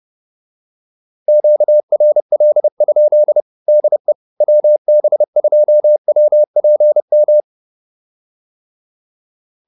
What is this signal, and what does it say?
Morse code, heard as a keyed tone, about 30 wpm, 590 Hz. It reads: QRL? DE WB2WPM